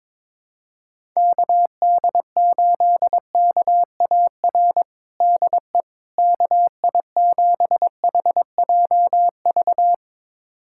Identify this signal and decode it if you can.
Morse code, heard as a keyed tone, about 22 wpm, 700 Hz. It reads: KD8KAR DE KI7HJV
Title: USA Callsign DE Callsign - 6 Characters - 22wpm